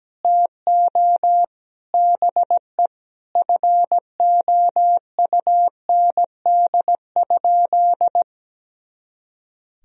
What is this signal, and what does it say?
Morse code, heard as a keyed tone, about 17 wpm, 700 Hz. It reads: TO BE FOUND?
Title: Sentences from Top 300 Words and Abbreviations 17wpm